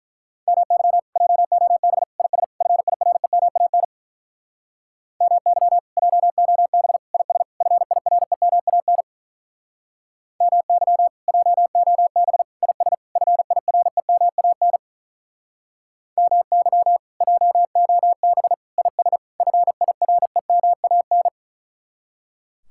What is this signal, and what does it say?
Morse code, heard as a keyed tone, about 46 wpm, 700 Hz. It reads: MY JOB IS FIREMAN MY JOB IS FIREMAN MY JOB IS FIREMAN MY JOB IS FIREMAN